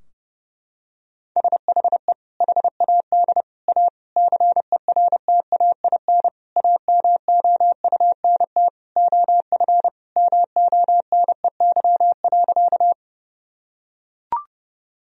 Morse code, heard as a keyed tone, about 30 wpm, 700 Hz. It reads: SHE HAD A CERTAIN AMOUNT OF MONEY.
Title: Sentences from Top 1000 Words - Review Part 4 - 30wpm